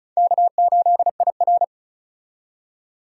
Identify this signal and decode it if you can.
Morse code, heard as a keyed tone, about 34 wpm, 700 Hz. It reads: K8IR
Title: USA Call Signs - 4 Characters - 35wpm